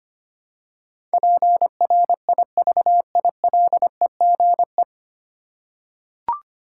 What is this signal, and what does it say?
Morse code, heard as a keyed tone, about 25 wpm, 700 Hz. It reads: PRIVILEGE
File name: Nine Letter Words - 25wpm